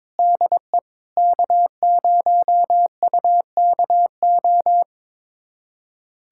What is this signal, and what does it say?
Morse code, heard as a keyed tone, about 21 wpm, 700 Hz. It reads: DE K0UKO
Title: USA Callsign DE Callsign - 5 Characters - 22wpm